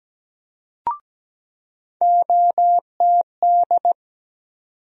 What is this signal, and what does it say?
Morse code, heard as a keyed tone, about 17 wpm, 700 Hz. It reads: OTD
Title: USA Call Sign Postfixes - 3 Letters 17wpm